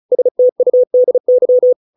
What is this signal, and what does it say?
Morse code, heard as a keyed tone, about 34 wpm, 500 Hz. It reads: STUDY